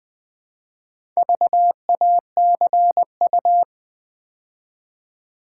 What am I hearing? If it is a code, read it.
Morse code, heard as a keyed tone, about 20 wpm, 690 Hz. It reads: VACU
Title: Build A Word - 2 Letters At A Time - 6-Letter Words - 20wpm